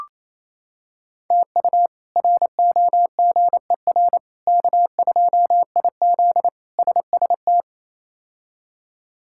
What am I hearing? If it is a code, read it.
Morse code, heard as a keyed tone, about 28 wpm, 700 Hz. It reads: TU ROGER K2IZ SST
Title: K1USN SST QSO Practice - 28wpm